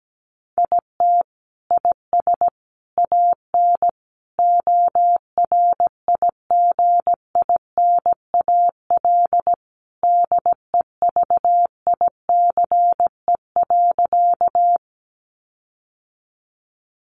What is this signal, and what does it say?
Morse code, heard as a keyed tone, about 17 wpm, 700 Hz. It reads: IT IS AN ORIGINAL DEVICE.
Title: Sentences from Top 1500 Words 17wpm